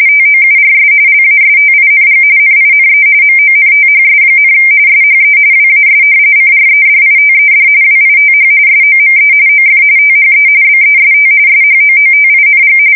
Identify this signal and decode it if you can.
RTTY, about 45.45 baud. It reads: TOURS LASTING NINE MONTHS, SHE WAS CREDITED WITH SOME OF THE BEST SHOOTING
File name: fsk170 / battleshipnj